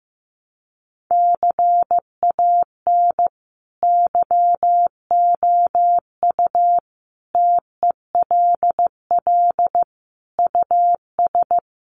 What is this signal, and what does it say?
Morse code, heard as a keyed tone, about 15 wpm, 700 Hz. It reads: CAN YOU TELL US
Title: Sets of 5 Words Encore2 15wpm